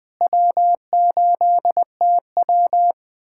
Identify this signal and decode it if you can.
Morse code, heard as a keyed tone, about 21 wpm, 700 Hz. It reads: W8TW
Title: USA Callsign DE Callsign - 4 Characters - 20wpm